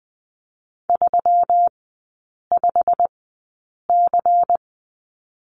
Morse code, heard as a keyed tone, about 20 wpm, spaced as for 8 wpm, 700 Hz. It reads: 35C
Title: 138 Lesson 23e - Letter f - Three Characters 20wpm